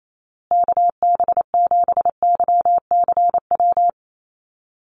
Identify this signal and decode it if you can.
Morse code, heard as a keyed tone, about 28 wpm, 700 Hz. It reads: KB7YCW
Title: USA Call Signs Repeated 2x 28wpm